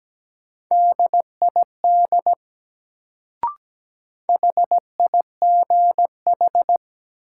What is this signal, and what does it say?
Morse code, heard as a keyed tone, about 17 wpm, 700 Hz. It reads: DID HIGH
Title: Sets of 2 Words - New Word Drill - 17wpm